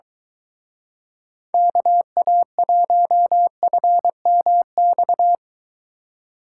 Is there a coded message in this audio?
Yes: Morse code from 1.5 to 5.3 s, heard as a keyed tone, about 23 wpm, 700 Hz. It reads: KA1FMX